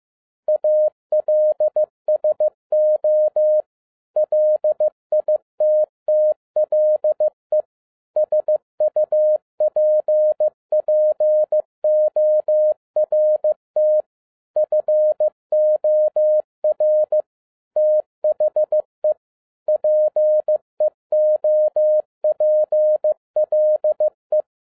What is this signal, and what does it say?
Morse code, heard as a keyed tone, about 15 wpm, 610 Hz. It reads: ALSO LITTLE SUPPORT FOR THE PEOPLE
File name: 2011-10-mp3 / n02-FIELD DAY CONTEST QRV FROM SAKAE VILLAGE 15wpm